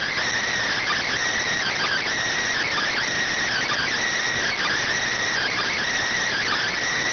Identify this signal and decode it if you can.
Emergency Alert System (SAME) tones (header bursts), heard as harsh buzzing bursts. It is heard from 0.5 to 6.2 s.